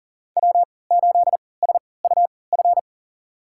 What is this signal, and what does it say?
Morse code, heard as a keyed tone, about 40 wpm, spaced as for 23 wpm, 700 Hz. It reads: W8SUF
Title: USA Call Signs - 3x Character Spacing - 40wpm